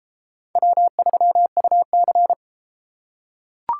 Morse code, heard as a keyed tone, about 31 wpm, 700 Hz. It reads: W3UC